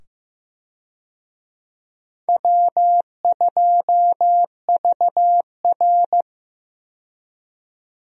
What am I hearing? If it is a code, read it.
Morse code, heard as a keyed tone, about 15 wpm, 700 Hz. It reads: W2VR
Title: USA Call Signs - 4 Characters - 15wpm